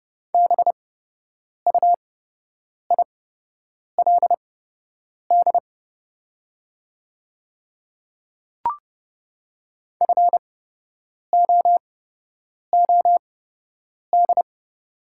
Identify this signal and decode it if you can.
Morse code, heard as a keyed tone, about 30 wpm, spaced as for 8 wpm, 700 Hz. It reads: BUILD FOOD
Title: 8x Character Spacing - Top 500 Words - 30wpm